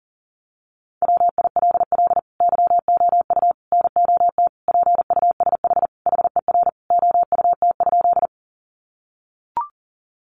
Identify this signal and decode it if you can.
Morse code, heard as a keyed tone, about 40 wpm, 700 Hz. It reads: WILL YOU NOT PUSH HER OUT?
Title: Sentences from Top 1300 Words 40wpm